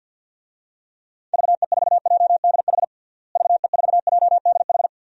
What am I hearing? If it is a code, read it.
Morse code, heard as a keyed tone, about 46 wpm, 700 Hz. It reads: VE4JDH VE4JDH
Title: DXCC-100 Call Signs - 6 Characters - Repeated 2x - 50wpm